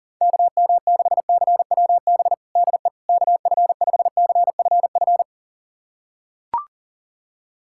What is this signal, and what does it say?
Morse code, heard as a keyed tone, about 40 wpm, 700 Hz. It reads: KM6CWB DE KF5CFF
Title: USA Callsign DE Callsign 40wpm